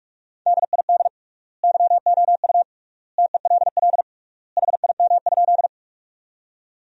Morse code, heard as a keyed tone, about 41 wpm, 700 Hz. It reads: DID YOU TELL HIM?
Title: 3x Word-Spacing - Sentences from Top 200 Words - 45wpm